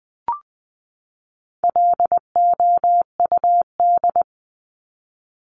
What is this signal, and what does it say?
Morse code, heard as a keyed tone, about 20 wpm, 700 Hz. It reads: LOUD
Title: Top 1751-2000 Words 20wpm